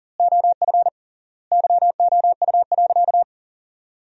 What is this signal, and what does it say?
Morse code, heard as a keyed tone, about 40 wpm, 700 Hz. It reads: OF YOU.
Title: 3x Word-Spacing - Sentences from Top 200 Words - 40wpm